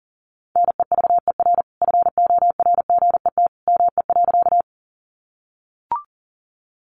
Morse code, heard as a keyed tone, about 40 wpm, 700 Hz. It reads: NEVER FORGET ME.
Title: Sentences from Top 1200 Words 40wpm